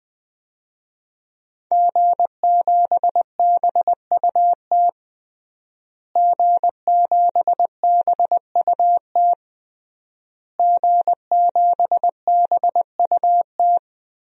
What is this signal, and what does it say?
Morse code, heard as a keyed tone, about 20 wpm, 700 Hz. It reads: G7BUT G7BUT G7BUT
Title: DXCC-100 Call Signs - 5 Characters - Repeated 3x - 20wpm